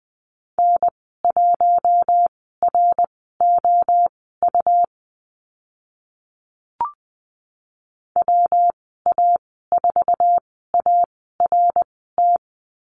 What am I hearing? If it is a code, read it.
Morse code, heard as a keyed tone, about 20 wpm, spaced as for 15 wpm, 700 Hz. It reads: N1ROU WA4ART
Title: 140 Lesson 23g - Letter f - Call Signs 20wpm